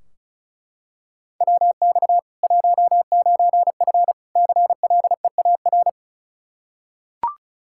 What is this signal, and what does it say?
Morse code, heard as a keyed tone, about 36 wpm, 700 Hz. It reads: WX 19F CLEAR
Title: Practice QSO - Exchange 2 - Info - 35wpm